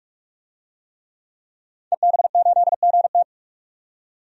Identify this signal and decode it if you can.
Morse code, heard as a keyed tone, about 42 wpm, 700 Hz. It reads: EB8GT
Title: DXCC-100 Call Signs - 5 Characters - 45wpm